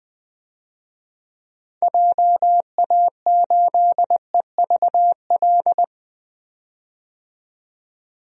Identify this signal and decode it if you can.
Morse code, heard as a keyed tone, about 20 wpm, 700 Hz. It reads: JA8EVL
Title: DXCC-100 Call Signs - 6 Characters - 20wpm